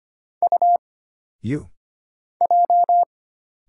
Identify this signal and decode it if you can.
Morse code, heard as a keyed tone, about 25 wpm, 700 Hz. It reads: U J
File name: Single Character (Letter:Number:Punctuation:Prosign) - Mind-Melt - 25wpm